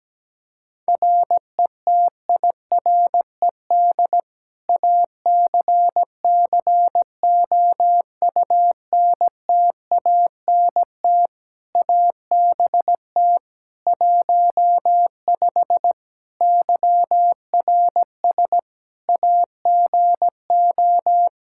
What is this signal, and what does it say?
Morse code, heard as a keyed tone, about 17 wpm, 700 Hz. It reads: RETIRED ACCOUNTANT ABT 15 YRS AGO
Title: Practice QSO - Exchange 3 - More Info - 17wpm